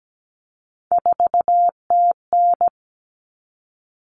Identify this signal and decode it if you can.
Morse code, heard as a keyed tone, about 17 wpm, 700 Hz. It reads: 4TN
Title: All Cut Numbers - 3-Digit 17wpm